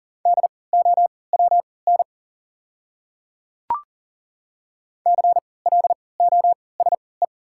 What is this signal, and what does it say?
Morse code, heard as a keyed tone, about 40 wpm, spaced as for 23 wpm, 700 Hz. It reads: DOWN CLOSE
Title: 3x Character Spacing - Top 500 Words - 40wpm